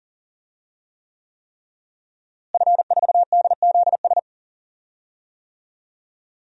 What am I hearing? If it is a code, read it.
Morse code, heard as a keyed tone, about 37 wpm, 700 Hz. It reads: F4DZS
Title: DXCC-100 Call Signs - 5 Characters - Repeated 3x - 40wpm